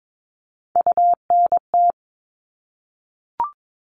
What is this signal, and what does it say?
Morse code, heard as a keyed tone, about 22 wpm, 700 Hz. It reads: UNT
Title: All Cut Numbers - 3-Digit 22wpm